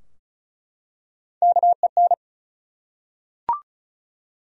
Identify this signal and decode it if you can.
Morse code, heard as a keyed tone, about 36 wpm, 700 Hz. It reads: KEN
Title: USA Common Names - Max 4 Letters - 35wpm